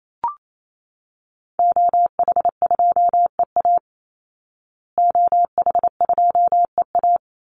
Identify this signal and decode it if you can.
Morse code, heard as a keyed tone, about 28 wpm, 700 Hz. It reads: OH2EA OH2EA
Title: Call Signs Repeated 3x - 28wpm